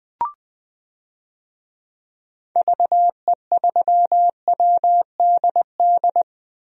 Morse code, heard as a keyed tone, about 20 wpm, 700 Hz. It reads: VE3WDD
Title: DXCC-100 Call Signs - 6 Characters - 20wpm